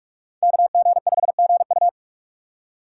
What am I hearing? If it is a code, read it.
Morse code, heard as a keyed tone, about 43 wpm, 700 Hz. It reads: KG5GU